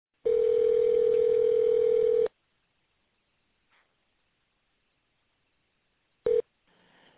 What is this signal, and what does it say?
Ringback tone (North America), 440 + 480 Hz.